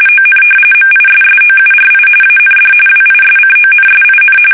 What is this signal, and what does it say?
RTTY, about 45.45 baud. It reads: ON MISSOURI AND WISCONSIN
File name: fsk850 / battleshipnj